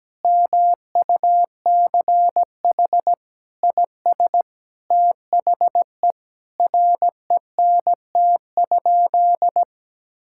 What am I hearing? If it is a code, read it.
Morse code, heard as a keyed tone, about 17 wpm, 700 Hz. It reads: MUCH IS THE RENT?
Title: Sentences from Top 1000 Words - Review Part 5 - 17wpm